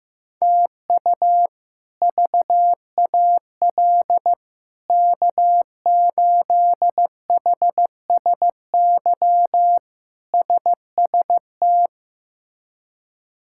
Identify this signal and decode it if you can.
Morse code, heard as a keyed tone, about 15 wpm, 700 Hz. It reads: TU VAL K8HSY SST
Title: K1USN SST QSO Practice - 15wpm